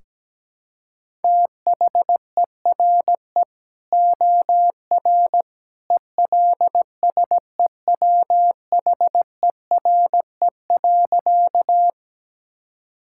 Morse code, heard as a keyed tone, about 17 wpm, 700 Hz. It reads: THERE OR ELSEWHERE.
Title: Sentences from Top 1600 Words 17wpm